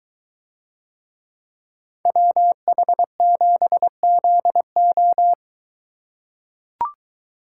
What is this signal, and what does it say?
Morse code, heard as a keyed tone, about 22 wpm, 700 Hz. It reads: WH7ZO